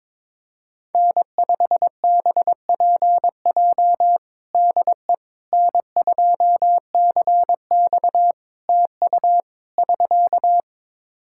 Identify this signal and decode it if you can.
Morse code, heard as a keyed tone, about 21 wpm, 700 Hz. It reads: N5BPJ DE N2CX TU <SK>
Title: K1USN SST QSO Practice 22wpm